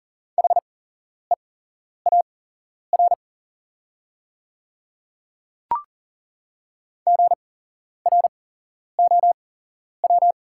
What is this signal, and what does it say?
Morse code, heard as a keyed tone, about 40 wpm, spaced as for 11 wpm, 700 Hz. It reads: HEAR GROW
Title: 8x Character Spacing - Top 300 Words - 40wpm